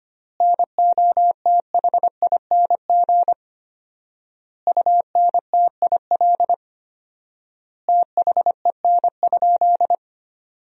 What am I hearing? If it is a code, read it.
Morse code, heard as a keyed tone, about 25 wpm, 700 Hz. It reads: NOTHING UNTIL THEN?
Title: 4x Word-Spacing - Sentences from Top 500 Words - 25wpm